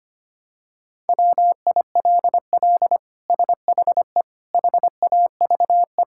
Morse code, heard as a keyed tone, about 25 wpm, 700 Hz. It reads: WILL SHE HAVE
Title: Sentences from Top 1000 Words - Review Part 6 - 25wpm